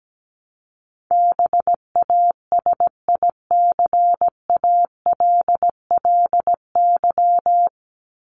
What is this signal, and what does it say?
Morse code, heard as a keyed tone, about 17 wpm, 700 Hz. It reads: BASICALLY
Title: Top 1251-1500 Words 17wpm